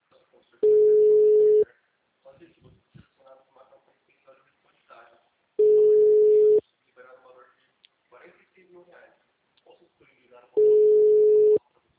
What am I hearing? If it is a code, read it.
Ringback tone (Europe, 425 Hz), with speech.
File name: out- -2010- - - .48